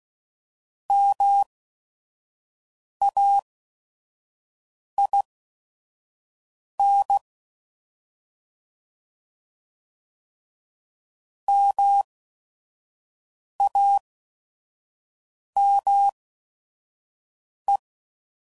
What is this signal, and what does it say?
Morse code, heard as a keyed tone, about 16 wpm, spaced as for 5 wpm, 800 Hz. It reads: MAIN MAME